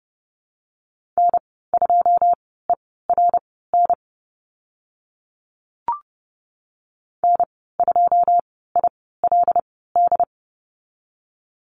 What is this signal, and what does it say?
Morse code, heard as a keyed tone, about 30 wpm, spaced as for 17 wpm, 700 Hz. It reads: N2ERN N2ILD
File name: Lesson 14g - Number 2 - Call Signs 30wpm